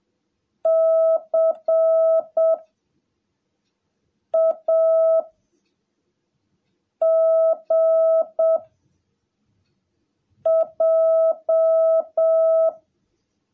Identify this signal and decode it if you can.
Morse code, heard as a keyed tone, about 7 wpm, 660 Hz. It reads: C A G J
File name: jh8cbh / 2021cw / cw17a